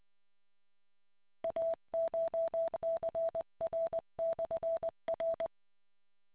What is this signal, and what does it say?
Morse code, heard as a keyed tone, about 20 wpm, 660 Hz. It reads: ACHCR/R